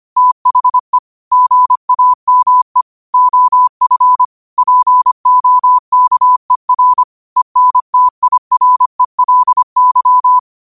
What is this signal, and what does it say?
Morse code, heard as a keyed tone, about 25 wpm, 1000 Hz. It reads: THE GAME OF POKER ENTIRELY
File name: qotdfiles / 25wpm